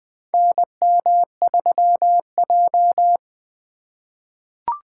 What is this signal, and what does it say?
Morse code, heard as a keyed tone, about 20 wpm, 700 Hz. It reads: NM3J
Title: USA Call Signs - 4 Characters - 4x Character Spacing - 20wpm